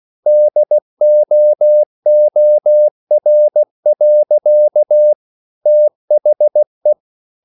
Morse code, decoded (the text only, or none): DOOR. THE